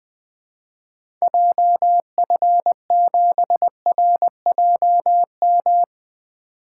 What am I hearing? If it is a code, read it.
Morse code, heard as a keyed tone, about 20 wpm, 700 Hz. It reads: JF7RJM
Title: 2023 POTA Activator Call Signs - 20wpm